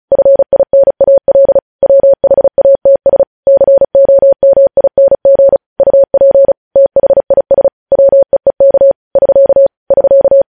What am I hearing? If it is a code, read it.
Morse code, heard as a keyed tone, about 36 wpm, 550 Hz. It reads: FINAL WHATS COMING UP THIS WEEK <SK> <SK>
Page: news headlines in morse code at 35 wpm